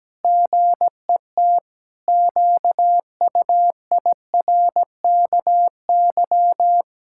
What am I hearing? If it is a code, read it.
Morse code, heard as a keyed tone, about 17 wpm, 700 Hz. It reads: GET QUIRKY
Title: Pangram Challenge v2 - 17wpm